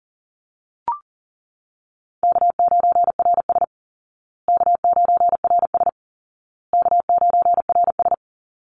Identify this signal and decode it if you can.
Morse code, heard as a keyed tone, about 40 wpm, 700 Hz. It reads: K9RS K9RS K9RS
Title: Call Signs Repeated 2x - 40wpm